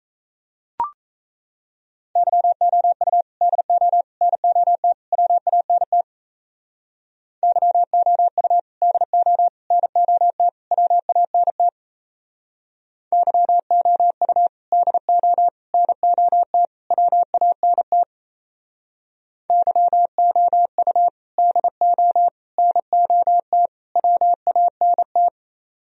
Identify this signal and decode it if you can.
Morse code, heard as a keyed tone, about 35 wpm, 700 Hz. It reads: YOU DO NOT WANT YOU DO NOT WANT YOU DO NOT WANT YOU DO NOT WANT